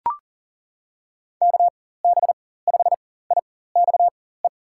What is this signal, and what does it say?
Morse code, heard as a keyed tone, about 40 wpm, spaced as for 19 wpm, 700 Hz. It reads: KB5IXE
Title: USA Call Signs - 4x Character Spacing - 40wpm